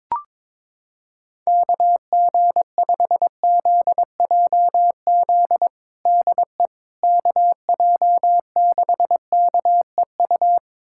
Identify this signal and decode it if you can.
Morse code, heard as a keyed tone, about 22 wpm, 700 Hz. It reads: KG5ZJZ DE KJ6KEU
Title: USA Callsign DE Callsign - 6 Characters - 22wpm